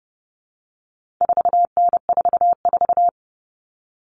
Morse code, heard as a keyed tone, about 30 wpm, 700 Hz. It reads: 4N44
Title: All Cut Numbers - 4-Digit 30wpm